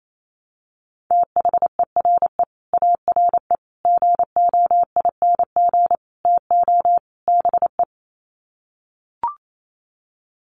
Morse code, heard as a keyed tone, about 28 wpm, 700 Hz. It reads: THERE ARE GOING TO BE
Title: Sets of 5 Words 28wpm